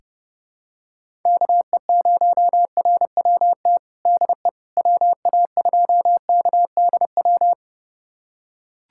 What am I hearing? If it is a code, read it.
Morse code, heard as a keyed tone, about 30 wpm, 700 Hz. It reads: KE0RWT DE WA2KDW
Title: USA Callsign DE Callsign - 6 Characters - 30wpm